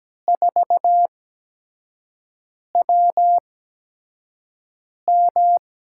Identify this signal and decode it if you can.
Morse code, heard as a keyed tone, about 17 wpm, spaced as for 5 wpm, 700 Hz. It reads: 4WM
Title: USA Call Signs - 8x Character Spacing - 17wpm